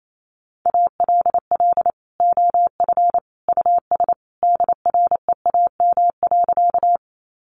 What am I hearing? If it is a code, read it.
Morse code, heard as a keyed tone, about 28 wpm, 700 Hz. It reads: ALL OF US DREAM.
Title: Sentences from Top 1500 Words 28wpm